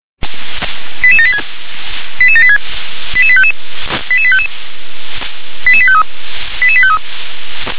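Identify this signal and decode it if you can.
Five-tone selective call (ZVEI), heard as quick runs of five tones, tones of about 70 ms. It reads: 88786 88786 88855 88855 88853 88853
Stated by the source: ZVEI five-tone selective call received on an SDR.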